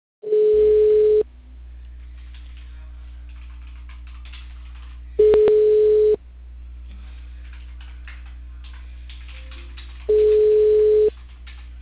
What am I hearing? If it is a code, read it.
Ringback tone (Europe, 425 Hz).